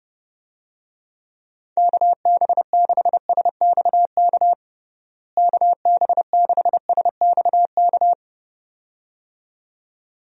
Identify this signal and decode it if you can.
Morse code, heard as a keyed tone, about 32 wpm, 700 Hz. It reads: KB6SXK KB6SXK
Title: USA Call Signs - 6 Characters - Repeated 2x - 30wpm